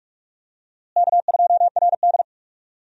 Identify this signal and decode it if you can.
Morse code, heard as a keyed tone, about 46 wpm, 700 Hz. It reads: K2RD
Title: USA Call Signs - 4 Characters - Repeated 2x - 45wpm